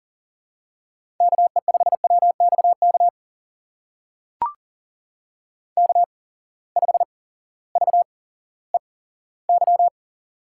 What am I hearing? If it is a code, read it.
Morse code, heard as a keyed tone, about 40 wpm, spaced as for 18 wpm, 700 Hz. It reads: KE5WXK K5VEY
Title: USA Call Signs - 8x Character Spacing - 40wpm